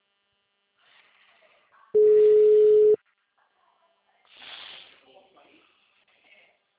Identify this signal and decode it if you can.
Ringback tone (Europe, 425 Hz), with speech.